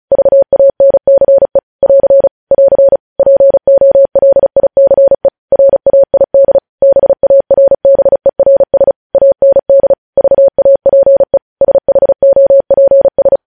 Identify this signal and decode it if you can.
Morse code, heard as a keyed tone, about 36 wpm, 550 Hz. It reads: VANCE <AR> <AR> POLICE RAID BARBERS AND VAPE SHOPS
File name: podcast / News headlines-1x-35x35